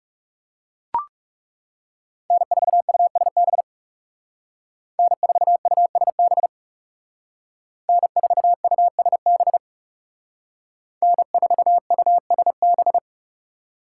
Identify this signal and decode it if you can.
Morse code, heard as a keyed tone, about 35 wpm, 700 Hz. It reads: N4USB N4USB N4USB N4USB
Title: Speed Racing - USA 1x3 Call Signs 30wpm